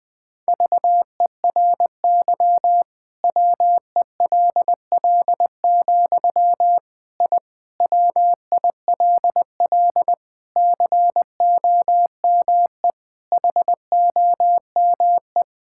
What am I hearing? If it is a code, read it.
Morse code, heard as a keyed tone, about 20 wpm, 700 Hz. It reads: VERY WELL, I WILL COME HOME
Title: Sentences from Top 1000 Words - Review Part 5 - 20wpm